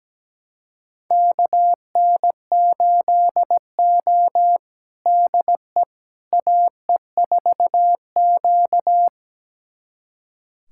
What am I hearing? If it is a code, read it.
Morse code, heard as a keyed tone, about 17 wpm, 700 Hz. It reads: KN8O DE AE4Q
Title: USA Callsign DE Callsign - 4 Characters - 17wpm